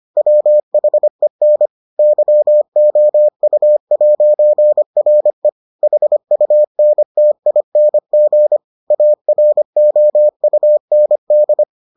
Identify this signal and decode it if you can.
Morse code, heard as a keyed tone, about 25 wpm, 600 Hz. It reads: WHEN YOU'RE HUNTING AROUND